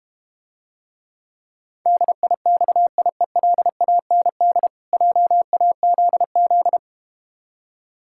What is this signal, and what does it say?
Morse code, heard as a keyed tone, about 32 wpm, 700 Hz. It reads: DIXIELAND JAZZ